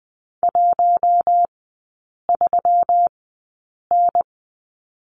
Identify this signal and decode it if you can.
Morse code, heard as a keyed tone, about 20 wpm, spaced as for 8 wpm, 700 Hz. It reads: 13N
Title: 121 Lesson 20e - Number 3 - Three Characters 20wpm